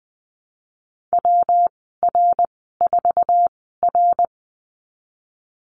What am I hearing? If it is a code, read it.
Morse code, heard as a keyed tone, about 20 wpm, spaced as for 15 wpm, 700 Hz. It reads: WR4R
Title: 122 Lesson 20g - Number 3 - Call Signs 20wpm